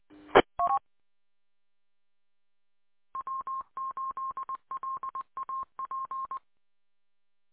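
Morse code, heard as a keyed tone, about 20 wpm, 1060 Hz. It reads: ET W8LAP
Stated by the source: Recorded on 146.620 MHz.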